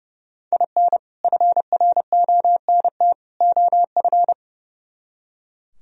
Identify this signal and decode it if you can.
Morse code, heard as a keyed tone, about 38 wpm, 700 Hz. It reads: IN FRONT OF